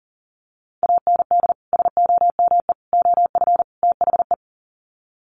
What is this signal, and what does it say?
Morse code, heard as a keyed tone, about 40 wpm, 700 Hz. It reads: AND SOME OF THE
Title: Sets of 4 Words 40wpm